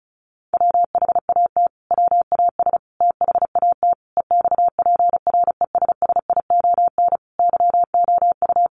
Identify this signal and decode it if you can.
Morse code, heard as a keyed tone, about 34 wpm, 700 Hz. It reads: WHAT WAS THAT EXPRESSION YOU
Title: Sentences from Top 1200 Words 35wpm